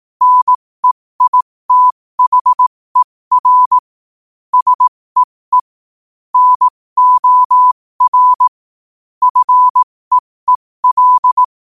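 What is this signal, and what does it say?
Morse code, heard as a keyed tone, about 18 wpm, spaced as for 15 wpm, 1000 Hz. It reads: NEITHER SEE NOR FEEL
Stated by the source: Morse code quote of the day sent at 15 words per minute.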